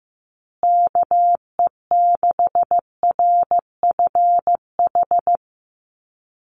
Morse code, heard as a keyed tone, about 15 wpm, 700 Hz. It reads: KE6RFH
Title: USA Call Signs Repeated 3x 15wpm